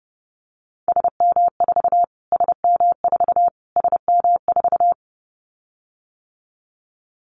Morse code, heard as a keyed tone, about 30 wpm, 700 Hz. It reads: SM4 SM4 SM4
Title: Callsign Prefixes for Top 25 DXCC - Repeated 3x - 30wpm